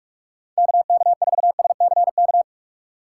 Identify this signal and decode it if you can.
Morse code, heard as a keyed tone, about 46 wpm, 700 Hz. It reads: KK4SCK
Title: USA Call Signs - 6 Characters - 4x Character Spacing - 45wpm